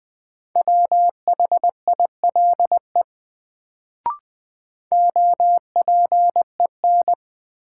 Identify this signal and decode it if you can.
Morse code, heard as a keyed tone, about 20 wpm, 700 Hz. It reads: WHILE OPEN
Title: Sentences from Top 300 Words - New Word Drill - 20wpm